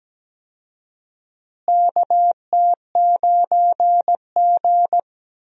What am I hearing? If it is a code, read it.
Morse code, heard as a keyed tone, about 17 wpm, 700 Hz. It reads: KT9G
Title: USA Call Signs - 4 Characters - Repeated 2x - 17wpm